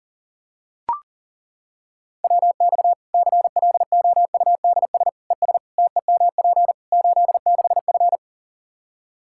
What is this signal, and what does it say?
Morse code, heard as a keyed tone, about 40 wpm, 700 Hz. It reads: WX CLOUDS ES TEMP 86F
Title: Practice QSO - Exchange 2 - Info - 40wpm